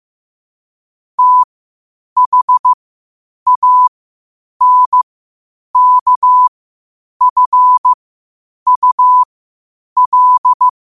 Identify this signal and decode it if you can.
Morse code, heard as a keyed tone, about 15 wpm, spaced as for 8 wpm, 1000 Hz. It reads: THANKFUL
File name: qotdfiles / 8wpm